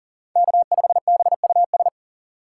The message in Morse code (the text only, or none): K5BUS